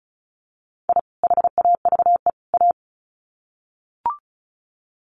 Morse code, heard as a keyed tone, about 34 wpm, 700 Hz. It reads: I HAVE A
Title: Sets of 3 Words 35wpm